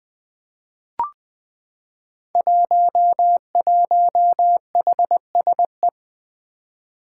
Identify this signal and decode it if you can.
Morse code, heard as a keyed tone, about 20 wpm, 700 Hz. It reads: 11HSE
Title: Five Challenging Letter-Number Combinations 20wpm